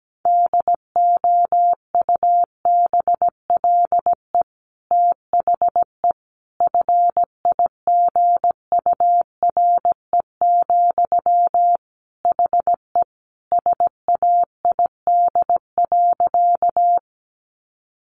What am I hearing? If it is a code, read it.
Morse code, heard as a keyed tone, about 17 wpm, 700 Hz. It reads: DOUBLE THE FIGURE, HE SAID.
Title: Sentences from Top 1400 Words 17wpm